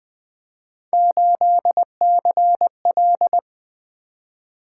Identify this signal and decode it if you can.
Morse code, heard as a keyed tone, about 20 wpm, 700 Hz. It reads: 8CL
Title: Three Letter-Number Combinations - 20wpm